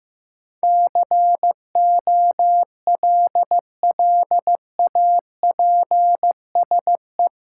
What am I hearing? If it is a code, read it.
Morse code, heard as a keyed tone, about 15 wpm, 700 Hz. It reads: COLLAPSE
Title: Sets of 5 Words Encore - New Word Drill - 15wpm